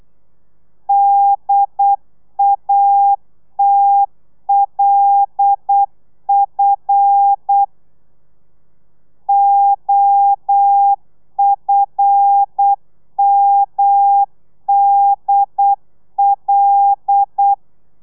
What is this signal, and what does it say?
Morse code, heard as a keyed tone, about 8 wpm, 800 Hz. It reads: DATLF OFMDL
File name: telegrafia / i2tfj / Lez5 35